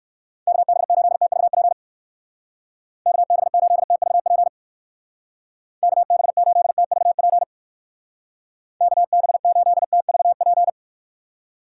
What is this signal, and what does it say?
Morse code, heard as a keyed tone, about 54 wpm, 700 Hz. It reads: KB8TVP KB8TVP KB8TVP KB8TVP
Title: Speed Racing - USA Call Signs 45wpm